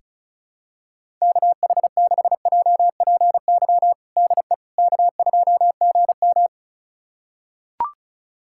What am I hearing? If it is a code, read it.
Morse code, heard as a keyed tone, about 34 wpm, 700 Hz. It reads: KH6JPY DE K2GM
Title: USA Callsign DE Callsign 35wpm